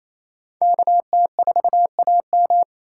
Morse code, heard as a keyed tone, about 35 wpm, 700 Hz. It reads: KT4AM